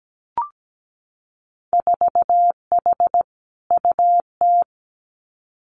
Morse code, heard as a keyed tone, about 17 wpm, 700 Hz. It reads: 4H UT
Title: Winter Field Day Exchanges Encore 17wpm